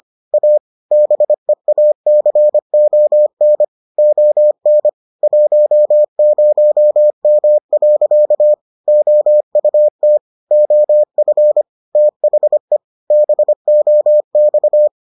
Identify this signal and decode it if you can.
Morse code, heard as a keyed tone, about 25 wpm, 600 Hz. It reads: A BEACON ON 10M. OUT OF THE BOX